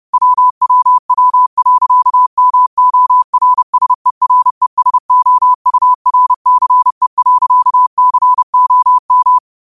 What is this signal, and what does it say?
Morse code, heard as a keyed tone, about 30 wpm, 1000 Hz. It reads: WWW.MORSERESOURCE.COM
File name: qotdfiles / 30wpm